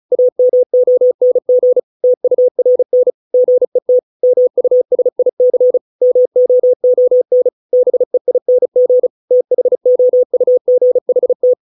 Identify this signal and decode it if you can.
Morse code, heard as a keyed tone, about 35 wpm, 500 Hz. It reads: AMONG TURN GET MUSIC MOON BEING THOUGHT
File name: 632 words / 632 Words 35WPM